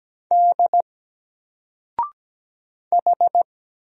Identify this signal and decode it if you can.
Morse code, heard as a keyed tone, about 17 wpm, 700 Hz. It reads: D H
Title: Single Character (Letter:Number:Punctuation:Prosign) 17wpm